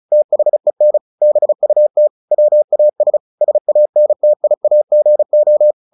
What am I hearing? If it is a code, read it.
Morse code, heard as a keyed tone, about 34 wpm, 600 Hz. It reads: THEN BUT WAS SANTIAGO
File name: the old man and the sea 2 35wpm